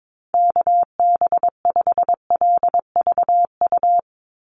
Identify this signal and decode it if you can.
Morse code, heard as a keyed tone, about 22 wpm, 700 Hz. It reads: KB5LVU
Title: USA Call Signs 22wpm